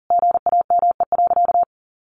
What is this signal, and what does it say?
Morse code, heard as a keyed tone, about 40 wpm, 700 Hz. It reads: GAME.
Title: Sentences from Top 1500 Words 40wpm